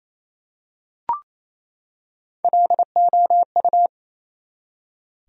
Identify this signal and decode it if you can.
Morse code, heard as a keyed tone, about 28 wpm, 700 Hz. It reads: LOU
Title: USA Common Names - Max 3 Letters - 28wpm